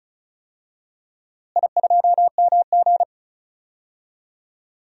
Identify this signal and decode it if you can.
Morse code, heard as a keyed tone, about 34 wpm, 700 Hz. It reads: I2MG
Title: DXCC-100 Call Signs - 4 Characters - Repeated 2x - 35wpm